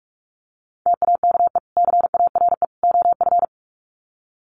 Morse code, heard as a keyed tone, about 45 wpm, 700 Hz. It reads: TAKE CARE OF
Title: Sets of 3 Words 45wpm